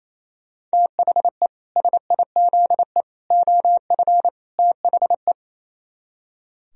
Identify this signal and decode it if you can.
Morse code, heard as a keyed tone, about 37 wpm, 700 Hz. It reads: THE SIZE OF THE